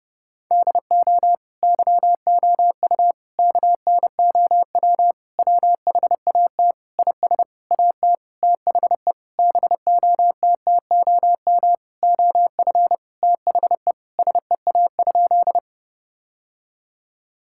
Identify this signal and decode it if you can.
Morse code, heard as a keyed tone, about 32 wpm, 700 Hz. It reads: DO YOU KNOW WHAT IS AT THE BOTTOM OF THE SEA?
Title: Sentences from Top 1000 Words - Review Part 6 - 30wpm